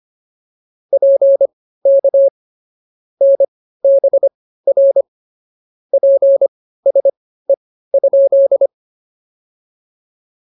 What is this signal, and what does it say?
Morse code, heard as a keyed tone, about 24 wpm, spaced as for 15 wpm, 550 Hz. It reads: PK NBR PSE?